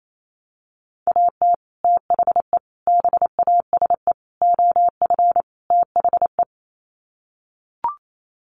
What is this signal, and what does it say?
Morse code, heard as a keyed tone, about 28 wpm, 700 Hz. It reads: AT THE BASE OF THE
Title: Sets of 5 Words 28wpm